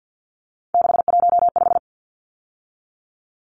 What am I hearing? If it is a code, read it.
Morse code, heard as a keyed tone, about 50 wpm, 700 Hz. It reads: 615
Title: Numbers - 3-Digit 50wpm